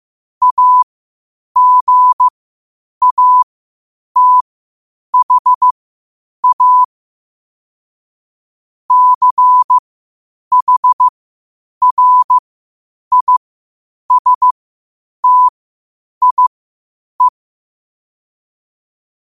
Morse code, heard as a keyed tone, about 15 wpm, spaced as for 8 wpm, 1000 Hz. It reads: AGATHA CHRISTIE
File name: qotdfiles / 8wpm